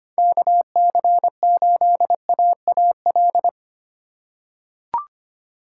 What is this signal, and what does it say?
Morse code, heard as a keyed tone, about 25 wpm, 700 Hz. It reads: KC8AAL
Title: USA Call Signs - 6 Characters - 2x Character Spacing - 25wpm